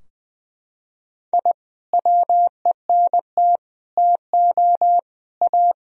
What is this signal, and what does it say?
Morse code, heard as a keyed tone, about 21 wpm, 700 Hz. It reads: I WENT TO A
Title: Sentences from Top 1000 Words - Review Part 3 - 20wpm